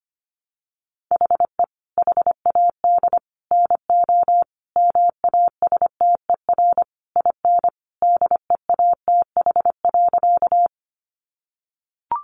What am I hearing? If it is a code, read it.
Morse code, heard as a keyed tone, about 25 wpm, 700 Hz. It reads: HE HAD NO MASTER IN DEATH.
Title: Sentences from Top 1300 Words 25wpm